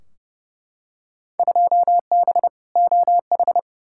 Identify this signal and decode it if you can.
Morse code, heard as a keyed tone, about 32 wpm, 700 Hz. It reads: 2B OH
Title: Field Day Exchanges Encore - 30wpm